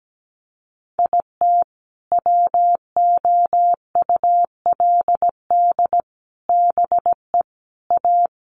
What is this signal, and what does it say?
Morse code, heard as a keyed tone, about 17 wpm, 700 Hz. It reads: IT WOULD BE A
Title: Sets of 5 Words Encore2 17wpm